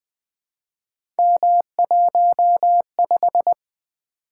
Morse code, heard as a keyed tone, about 20 wpm, 700 Hz. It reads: M15
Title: Three Letter-Number Combinations - 20wpm